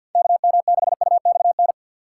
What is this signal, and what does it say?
Morse code, heard as a keyed tone, about 48 wpm, 700 Hz. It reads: KM6UXN
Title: USA Call Signs - 6x Character Spacing - 50wpm